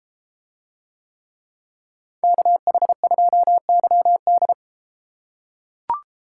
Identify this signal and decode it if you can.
Morse code, heard as a keyed tone, about 31 wpm, 700 Hz. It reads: KH2YD